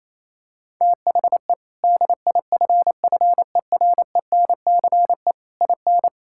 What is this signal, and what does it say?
Morse code, heard as a keyed tone, about 27 wpm, 700 Hz. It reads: THE DIFFERENCE IN
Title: Sentences from Top 1600 Words 28wpm